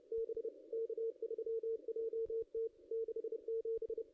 Morse code, heard as a keyed tone, about 33 wpm, 440 Hz. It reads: B K3JT 67
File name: contests / ss-cw-k5tr-op-n5rz -0400